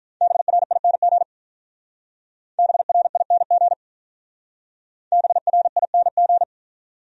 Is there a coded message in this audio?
Yes: Morse code from 0.2 to 6.4 s, heard as a keyed tone, about 45 wpm, 700 Hz. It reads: BRING BRING BRING